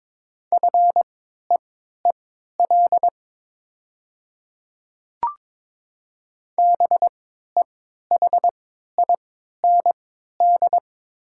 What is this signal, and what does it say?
Morse code, heard as a keyed tone, about 22 wpm, spaced as for 12 wpm, 700 Hz. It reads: FEEL BEHIND
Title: 3x Character Spacing - Top 500 Words - 22wpm